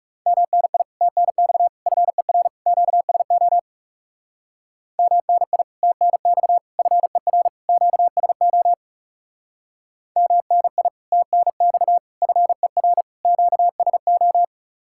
Morse code, heard as a keyed tone, about 37 wpm, 700 Hz. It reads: MNI TNX FER QSO MNI TNX FER QSO MNI TNX FER QSO
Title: Speed-Racing Common QSO Phrases - 30wpm